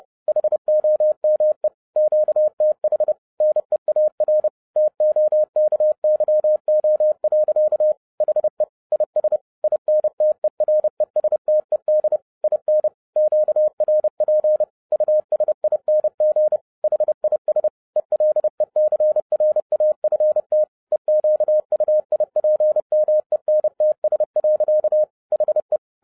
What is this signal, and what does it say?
Morse code, heard as a keyed tone, about 30 wpm, 610 Hz. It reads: HOME QTH NEAR TOKYO. HE IS INTERESTED IN QRP USING HIS ELECRAFT EQUIPMENTS. HE